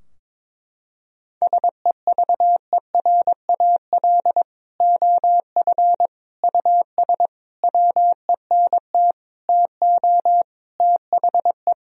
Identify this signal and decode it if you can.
Morse code, heard as a keyed tone, about 22 wpm, 700 Hz. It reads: SEVERAL OF US WENT TO THE
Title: Sentences from Top 1000 Words - Review Part 2 - 22wpm